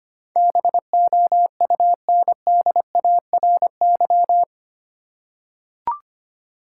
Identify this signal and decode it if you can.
Morse code, heard as a keyed tone, about 25 wpm, 700 Hz. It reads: BOUNDARY
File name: Eight Letter Words - 25wpm